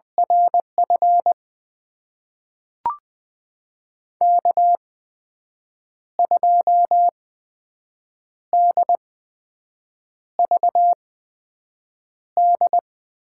Morse code, heard as a keyed tone, about 20 wpm, spaced as for 6 wpm, 700 Hz. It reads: RF K2DVD
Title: USA Call Signs - 5 Characters - 8x Character Spacing - 20wpm